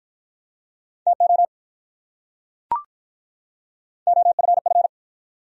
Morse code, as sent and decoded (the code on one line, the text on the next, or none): - --- / --- ..-. ..-.
TO OFF